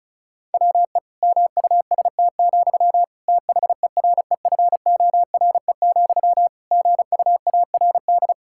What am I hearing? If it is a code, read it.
Morse code, heard as a keyed tone, about 36 wpm, 700 Hz. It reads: WE MUST, THEREFORE, GUARD
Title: Sentences from Top 1600 Words 35wpm